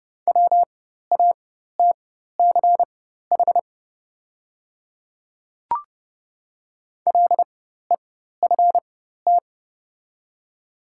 Morse code, heard as a keyed tone, about 30 wpm, spaced as for 14 wpm, 700 Hz. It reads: WATCH LEFT
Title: 4x Character Spacing - Top 500 Words - 30wpm